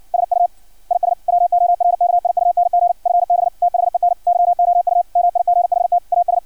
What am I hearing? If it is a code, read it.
Morse code, heard as a keyed tone, about 56 wpm, 700 Hz. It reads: SU IS WORKIUNG FB THEN YOU MIGHT AS
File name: archived / CRSnet